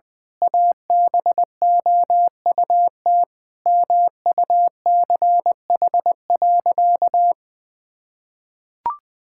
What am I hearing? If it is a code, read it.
Morse code, heard as a keyed tone, about 20 wpm, 700 Hz. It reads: ABOUT MUCH.
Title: Sentences from Top 1000 Words - Review Part 6 - 20wpm